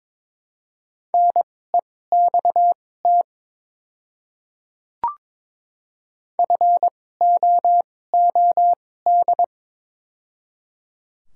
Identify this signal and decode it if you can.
Morse code, heard as a keyed tone, about 22 wpm, spaced as for 16 wpm, 700 Hz. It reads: NEXT FOOD
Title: 2x Character Spacing - Top 300 Words - 22wpm